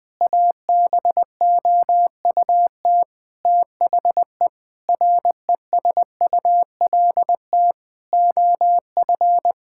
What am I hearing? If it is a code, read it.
Morse code, heard as a keyed tone, about 20 wpm, 700 Hz. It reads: ABOUT THE RESULT OF
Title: Sentences from Top 1000 Words - Review Part 5 - 20wpm